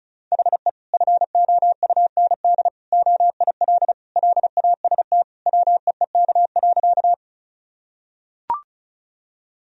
Morse code, heard as a keyed tone, about 34 wpm, 700 Hz. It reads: HE FOUND OIL LAST WEEK.
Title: Sentences from Top 1000 Words - Review Part 4 - 35wpm